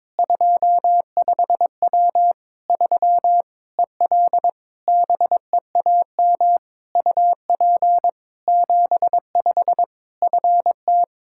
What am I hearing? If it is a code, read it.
Morse code, heard as a keyed tone, about 22 wpm, 700 Hz. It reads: 25W 3 EL BEAM UP 75 FT
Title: K1USN SST QSO Practice 22wpm